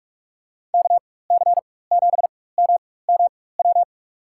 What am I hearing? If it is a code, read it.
Morse code, heard as a keyed tone, about 44 wpm, spaced as for 21 wpm, 700 Hz. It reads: KC7MMW
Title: USA Call Signs - 4x Character Spacing - 45wpm